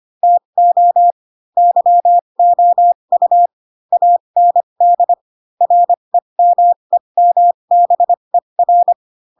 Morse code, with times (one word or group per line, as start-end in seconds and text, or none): 0.2-1.1 TO
1.6-3.5 YOU
3.9-5.1 AND
5.6-8.9 REMEMBER